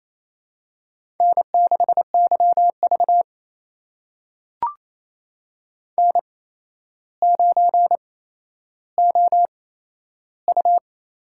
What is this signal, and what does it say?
Morse code, heard as a keyed tone, about 28 wpm, spaced as for 12 wpm, 700 Hz. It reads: N6YV N9OU
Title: USA Call Signs - 8x Character Spacing - 28wpm